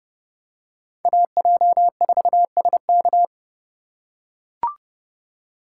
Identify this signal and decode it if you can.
Morse code, heard as a keyed tone, about 32 wpm, 700 Hz. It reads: AJ4SK